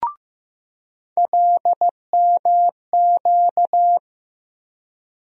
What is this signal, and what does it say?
Morse code, heard as a keyed tone, about 15 wpm, 700 Hz. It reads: LMQ
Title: USA Call Sign Postfixes - 3 Letters 15wpm